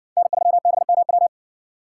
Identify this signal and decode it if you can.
Morse code, heard as a keyed tone, about 60 wpm, 700 Hz. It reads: N3BMW